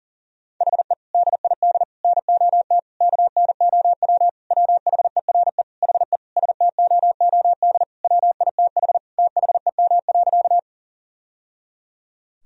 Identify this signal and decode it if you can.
Morse code, heard as a keyed tone, about 40 wpm, 700 Hz. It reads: HE DID NOT KNOW WHERE HE STOOD WITH THEM.
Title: Sentences from Top 1000 Words - Review Part 2 - 40wpm